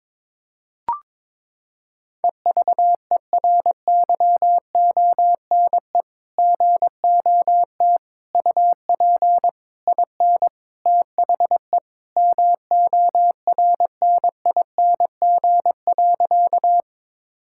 Morse code, heard as a keyed tone, about 22 wpm, 700 Hz. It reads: EVERYONE GOT UP IN THE MORNING.
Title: Sentences from Top 1000 Words - Review Part 5 - 22wpm